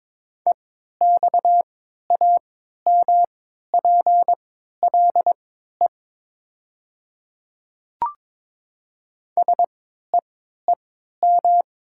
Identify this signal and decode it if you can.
Morse code, heard as a keyed tone, about 22 wpm, spaced as for 12 wpm, 690 Hz. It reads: EXAMPLE SEEM
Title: 3x Character Spacing - Top 300 Words - 22wpm